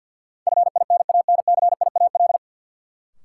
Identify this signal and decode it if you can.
Morse code, heard as a keyed tone, about 51 wpm, 700 Hz. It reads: FINANCIAL
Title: Sentences from Top 1000 Words - Review Part 5 - New Word Drill - 50wpm